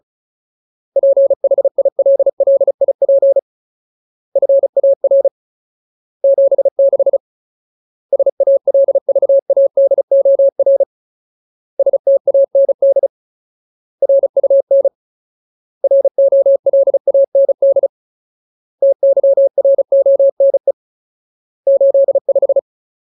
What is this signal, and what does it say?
Morse code, heard as a keyed tone, about 34 wpm, 550 Hz. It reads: PHILLIP FAR 76 SALVADOR STAND RUN ROLAND TYRONE 85
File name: Ultimate Word List / 35WPM Ultimate